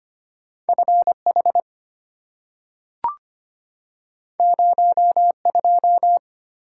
Morse code, heard as a keyed tone, about 25 wpm, 700 Hz. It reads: FH 02
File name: Two Letter-Number Combinations - 25wpm